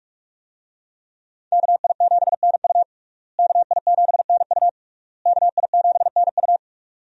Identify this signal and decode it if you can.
Morse code, heard as a keyed tone, about 41 wpm, 700 Hz. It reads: KI7NU KI7NU KI7NU
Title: USA Call Signs - 5 Characters - Repeated 3x - 45wpm